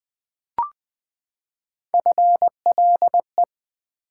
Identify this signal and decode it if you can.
Morse code, heard as a keyed tone, about 20 wpm, 700 Hz. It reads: FLE
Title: Build A Word - 3 Letters At A Time - 8-Letter Words - 20wpm